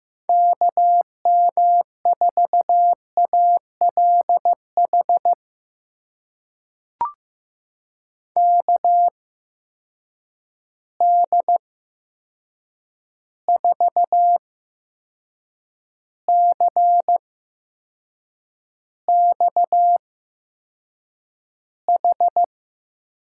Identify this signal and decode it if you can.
Morse code, heard as a keyed tone, about 15 wpm, spaced as for 6 wpm, 700 Hz. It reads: KM4ALH KD4CXH
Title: USA Call Signs - 6 Characters - 8x Character Spacing - 15wpm